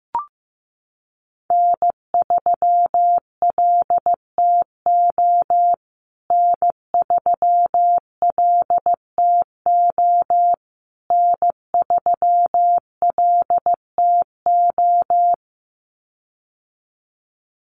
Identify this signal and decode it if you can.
Morse code, heard as a keyed tone, about 15 wpm, 700 Hz. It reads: N3LTO N3LTO N3LTO
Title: USA Call Signs Repeated 3x 15wpm